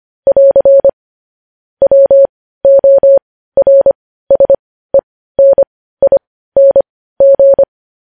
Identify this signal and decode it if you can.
Morse code, heard as a keyed tone, about 25 wpm, spaced as for 15 wpm, 550 Hz. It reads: <AR> WORSENING
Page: news headlines in morse code at 15 wpm